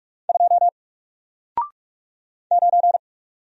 Morse code, heard as a keyed tone, about 44 wpm, 700 Hz. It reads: 2 9
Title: Single Character (Letter:Number:Punctuation:Prosign) 45wpm